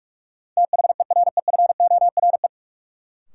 Morse code, heard as a keyed tone, about 46 wpm, 700 Hz. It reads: THEREFORE
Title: Sentences from Top 800 Words - New Word Drill - 45wpm